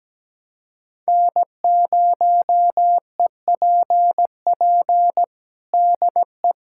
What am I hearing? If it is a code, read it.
Morse code, heard as a keyed tone, about 17 wpm, 700 Hz. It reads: N0EPP DE
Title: USA Callsign DE Callsign - 5 Characters - 17wpm